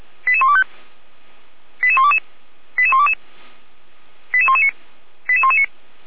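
Five-tone selective call (ZVEI), heard as quick runs of five tones, tones of about 70 ms. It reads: 88125 88122 88122 88119 88119
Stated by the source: ZVEI five-tone selective call received on an SDR, archived by Feuerwehr Orlamünde (f85360).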